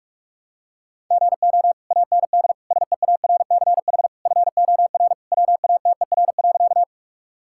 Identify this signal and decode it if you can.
Morse code, heard as a keyed tone, about 46 wpm, 690 Hz. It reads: GO AND SEARCH FOR WATER.
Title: Sentences from Top 1000 Words - Review Part 5 - 45wpm